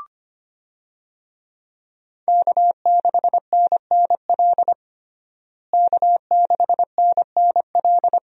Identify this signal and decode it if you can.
Morse code, heard as a keyed tone, about 25 wpm, 700 Hz. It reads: K6NNL K6NNL
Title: USA Call Signs - 5 Characters - Repeated 3x - 25wpm